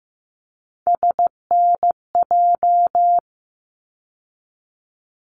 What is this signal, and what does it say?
Morse code, heard as a keyed tone, about 15 wpm, 700 Hz. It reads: SNJ
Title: ARRL Sections 15wpm